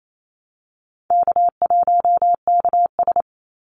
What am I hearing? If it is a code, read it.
Morse code, heard as a keyed tone, about 28 wpm, 700 Hz. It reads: K1KS
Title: Easy Call Signs Repeated 3x - 28wpm